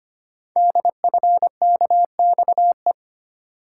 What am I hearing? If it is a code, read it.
Morse code, heard as a keyed tone, about 25 wpm, 700 Hz. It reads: DFKXE
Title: Code Groups - 5 Letters - 25wpm